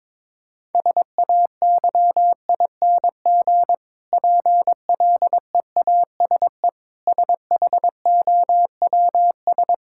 Morse code, heard as a keyed tone, about 22 wpm, 700 Hz. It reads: SAYING PLEASE SHOWS
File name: Difficult Sentences with Hints - 22wpm